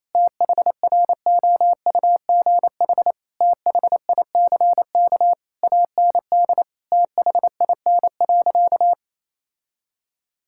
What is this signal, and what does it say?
Morse code, heard as a keyed tone, about 28 wpm, 700 Hz. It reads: THROUGH THICK AND THIN.
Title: Sentences from Top 1600 Words 28wpm